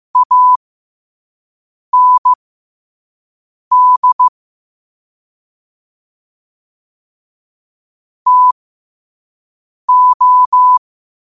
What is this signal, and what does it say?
Morse code, heard as a keyed tone, about 15 wpm, spaced as for 5 wpm, 1000 Hz. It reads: AND TO